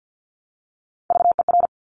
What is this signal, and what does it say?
Morse code, heard as a keyed tone, about 50 wpm, 690 Hz. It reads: VER
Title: Top Three Letter Combinations in Words 50wpm